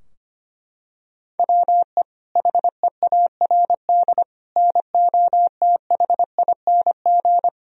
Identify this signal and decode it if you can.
Morse code, heard as a keyed tone, about 25 wpm, 700 Hz. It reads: WE HEARD NOTHING
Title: Sentences from Top 1000 Words - Review Part 3 - 25wpm